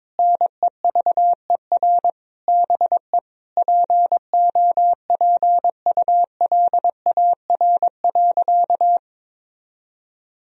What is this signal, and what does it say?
Morse code, heard as a keyed tone, about 22 wpm, 700 Hz. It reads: NEVER BE POPULAR.
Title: Sentences from Top 1000 Words - Review Part 3 - 22wpm